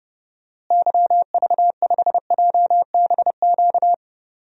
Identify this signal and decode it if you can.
Morse code, heard as a keyed tone, about 32 wpm, 700 Hz. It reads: YV5JBQ
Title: DXCC-100 Call Signs - 6 Characters - Repeated 2x - 30wpm